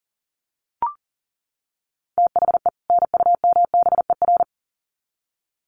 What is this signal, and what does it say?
Morse code, heard as a keyed tone, about 37 wpm, 700 Hz. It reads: THE NUMBER
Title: Sets of 2 Words 40wpm